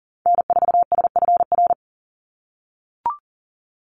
Morse code, heard as a keyed tone, about 39 wpm, 700 Hz. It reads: N4SFR
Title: NA SOTA Activator Callsigns 40wpm